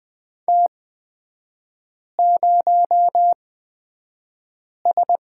Morse code, heard as a keyed tone, about 20 wpm, spaced as for 5 wpm, 700 Hz. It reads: T0S
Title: 250 Lesson 40e - BK - Three Characters 20wpm